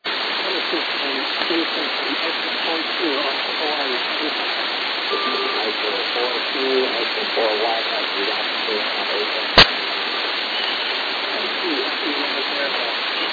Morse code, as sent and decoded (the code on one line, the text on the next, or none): none